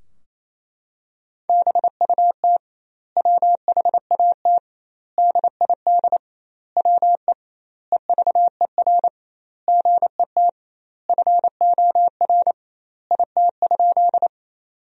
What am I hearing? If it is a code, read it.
Morse code, heard as a keyed tone, about 28 wpm, 700 Hz. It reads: BUT WHAT DID WE EVER GET FOR IT?
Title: 2x Word-Spacing - Sentences from Top 400 Words - 28wpm